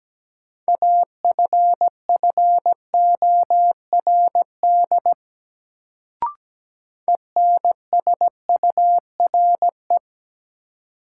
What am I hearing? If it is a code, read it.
Morse code, heard as a keyed tone, about 17 wpm, 700 Hz. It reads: AFFORD ENSURE
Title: Sentences from Top 1100 Words - New Word Drill Alt - 17wpm